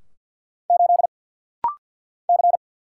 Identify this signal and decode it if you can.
Morse code, heard as a keyed tone, about 50 wpm, 690 Hz. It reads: 8 C
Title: Single Letter-Number Alt - Rapid-Fire 50wpm